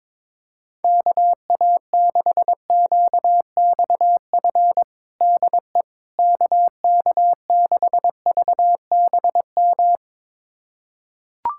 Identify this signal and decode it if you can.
Morse code, heard as a keyed tone, about 22 wpm, 700 Hz. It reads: KA6QXF DE KK6VBM
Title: USA Callsign DE Callsign - 6 Characters - 22wpm